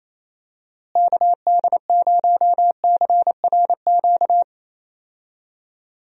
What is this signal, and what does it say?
Morse code, heard as a keyed tone, about 28 wpm, 700 Hz. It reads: KD0CRQ
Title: USA Call Signs - 6 Characters - Repeated 3x - 28wpm